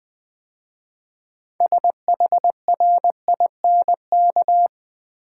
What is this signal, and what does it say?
Morse code, heard as a keyed tone, about 20 wpm, 700 Hz. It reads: SHRINK
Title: 6 Letter Words - First 3 Letters Given - 20wpm